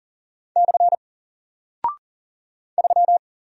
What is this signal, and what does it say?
Morse code, heard as a keyed tone, about 40 wpm, 700 Hz. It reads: / 3
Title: Single Character (Letter:Number:Punctuation:Prosign) Alt 40wpm